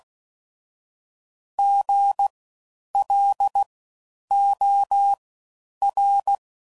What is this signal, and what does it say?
Morse code, heard as a keyed tone, about 16 wpm, spaced as for 9 wpm, 800 Hz. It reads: GLOR
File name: Suffixes / Special INGs / Fast Ings